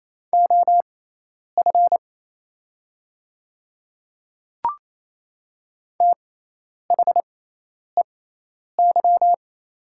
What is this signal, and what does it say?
Morse code, heard as a keyed tone, about 27 wpm, spaced as for 10 wpm, 700 Hz. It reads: OF THEY
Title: 6x Character Spacing - Top 100 Words - 28wpm